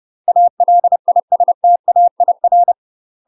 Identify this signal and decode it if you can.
Morse code, heard as a keyed tone, about 30 wpm, 700 Hz. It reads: ALISTAIR
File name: UK MASTER QSOs TRAINING / 30WPM / UK MASTER QSOs 0029